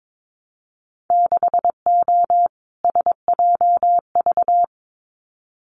Morse code, heard as a keyed tone, about 22 wpm, 700 Hz. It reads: 6O SJV
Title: Winter Field Day Exchanges Encore 22wpm